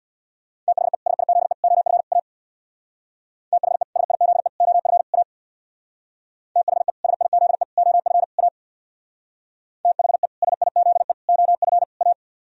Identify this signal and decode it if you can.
Morse code, heard as a keyed tone, about 62 wpm, 700 Hz. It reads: THE SIZE OF A THE SIZE OF A THE SIZE OF A THE SIZE OF A
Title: Speed-Racing - Sets of 4 Words 50wpm